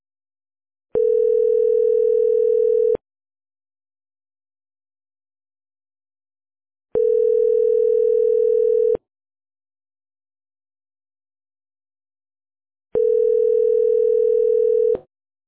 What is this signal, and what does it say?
Ringback tone (North America), 440 + 480 Hz.